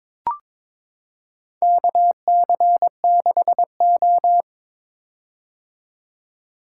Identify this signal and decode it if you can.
Morse code, heard as a keyed tone, about 22 wpm, 700 Hz. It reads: KC6O
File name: practice / 22wpm / Four Letter-Number Combinations - 22wpm